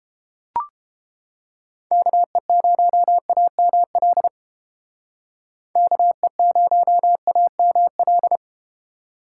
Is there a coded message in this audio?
Yes: Morse code from 1.9 to 8.4 s, heard as a keyed tone, about 31 wpm, 700 Hz. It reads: KE0AML KE0AML